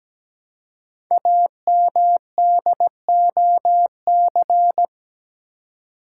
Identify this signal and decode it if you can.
Morse code, heard as a keyed tone, about 17 wpm, 700 Hz. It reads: AMDOC
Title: Code Groups - 5 Letters - 17wpm